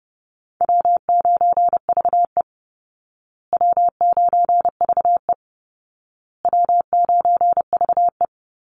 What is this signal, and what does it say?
Morse code, heard as a keyed tone, about 30 wpm, 700 Hz. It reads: W9VE W9VE W9VE
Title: Easy Call Signs Repeated 3x - 30wpm